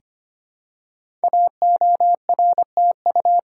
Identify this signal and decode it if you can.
Morse code, heard as a keyed tone, about 25 wpm, 700 Hz. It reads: AORTU
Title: Code Groups - 5 Letters - 25wpm